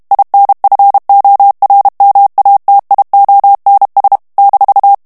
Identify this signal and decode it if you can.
Morse code, heard as a keyed tone, about 32 wpm, 800 Hz. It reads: INFORMATIONS -
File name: ALLEMAND texte clair 1 vitesse 30 mots minute